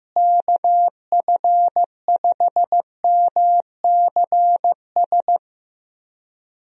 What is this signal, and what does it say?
Morse code, heard as a keyed tone, about 15 wpm, 700 Hz. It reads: KF5MCS
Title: USA Call Signs - 6 Characters - Repeated 2x - 15wpm